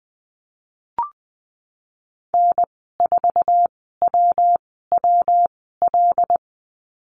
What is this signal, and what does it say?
Morse code, heard as a keyed tone, about 20 wpm, spaced as for 14 wpm, 700 Hz. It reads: N4WWL
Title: 122 Lesson 20g - Number 3 - Call Signs 20wpm